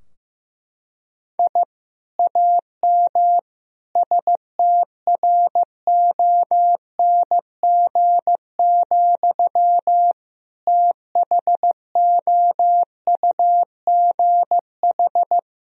Morse code, heard as a keyed tone, about 15 wpm, 700 Hz. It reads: I AM STRONG, THOUGH
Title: Sentences from Top 1000 Words - Review Part 3 - 15wpm